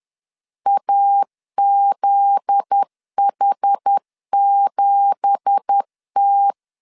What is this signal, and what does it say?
Morse code, heard as a keyed tone, about 10 wpm, 800 Hz. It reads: AZH7T
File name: CW / DET / 1 9 1500C S B 10wpm 4439